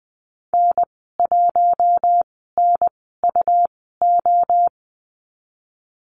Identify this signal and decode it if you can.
Morse code, heard as a keyed tone, about 20 wpm, spaced as for 15 wpm, 700 Hz. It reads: N1NUO
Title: 154 Lesson 25g - Letter p - Call Signs 20wpm